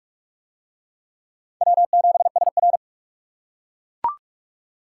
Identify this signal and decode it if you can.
Morse code, heard as a keyed tone, about 42 wpm, 700 Hz. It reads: W7SR